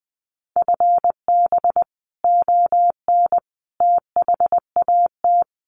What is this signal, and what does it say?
Morse code, heard as a keyed tone, about 20 wpm, 700 Hz. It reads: FB ON THAT
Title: Common QSO Phrases 20wpm